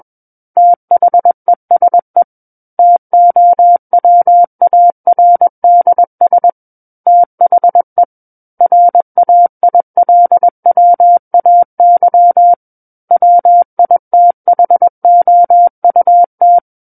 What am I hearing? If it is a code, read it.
Morse code, heard as a keyed tone, about 21 wpm, 700 Hz. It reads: THESE TOWARDS THE RAILWAY WITHOUT
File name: Book One / 12-Chapter 12 - 21 WPM